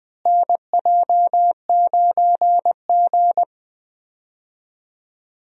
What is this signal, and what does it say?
Morse code, heard as a keyed tone, about 21 wpm, 700 Hz. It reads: NJ9G
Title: USA Call Signs - 4 Characters - Repeated 2x - 20wpm